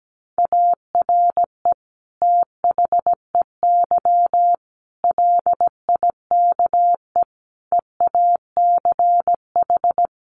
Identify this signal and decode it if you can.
Morse code, heard as a keyed tone, about 17 wpm, 700 Hz. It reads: ARE THEY LIKE EACH
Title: Sentences from Top 200 Words 17wpm